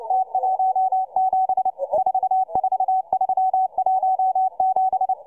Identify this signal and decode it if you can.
Morse code, heard as a keyed tone, about 29 wpm, 750 Hz, with radio noise. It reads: T17 44317